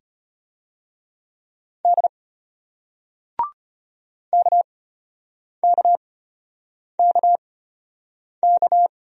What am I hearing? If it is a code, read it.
Morse code, heard as a keyed tone, about 33 wpm, spaced as for 8 wpm, 700 Hz. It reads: D KKKK